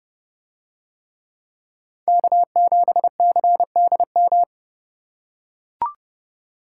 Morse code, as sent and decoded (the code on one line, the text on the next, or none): -.- --... -.-. -.. --
K7CDM